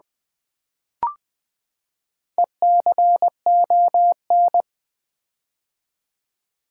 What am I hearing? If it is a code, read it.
Morse code, heard as a keyed tone, about 20 wpm, 700 Hz. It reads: ECON
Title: Build A Word - 4 Letters At A Time - 8-Letter Words - 20wpm